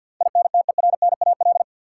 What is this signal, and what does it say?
Morse code, heard as a keyed tone, about 51 wpm, 700 Hz. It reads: INTERNAL
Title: Sentences from Top 1400 Words - New Word Drill Alt - 50wpm